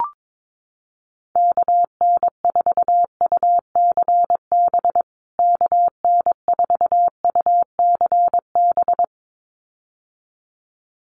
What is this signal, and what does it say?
Morse code, heard as a keyed tone, about 22 wpm, 700 Hz. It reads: KN4UCB KN4UCB
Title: USA Call Signs Repeated 2x 22wpm